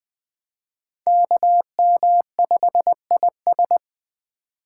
Morse code, heard as a keyed tone, about 21 wpm, 700 Hz. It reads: KM5IS
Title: Speed Racing - USA 2x2 Call Signs 17wpm